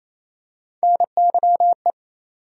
Morse code, heard as a keyed tone, about 28 wpm, 700 Hz. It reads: NYE
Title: USA Call Sign Postfixes - 3 Letters 28wpm